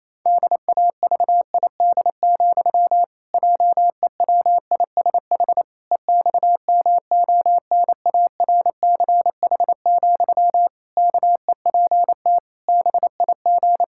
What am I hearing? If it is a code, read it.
Morse code, heard as a keyed tone, about 28 wpm, 700 Hz. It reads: DAVID, JEWISH EXMONARCH, KEPT BIG
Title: Pangram Challenge v2 - 28wpm